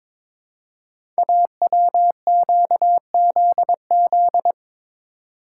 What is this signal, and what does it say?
Morse code, heard as a keyed tone, about 22 wpm, 700 Hz. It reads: AWQZZ
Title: Code Groups - 5 Letters - 22wpm